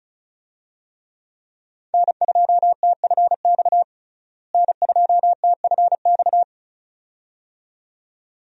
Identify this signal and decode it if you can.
Morse code, heard as a keyed tone, about 34 wpm, 700 Hz. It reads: N2TFX N2TFX
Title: USA Call Signs - 5 Characters - Repeated 2x - 35wpm